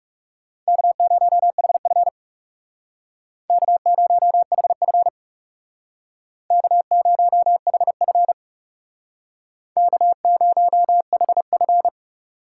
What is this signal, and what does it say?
Morse code, heard as a keyed tone, about 39 wpm, 700 Hz. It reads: K0HF K0HF K0HF K0HF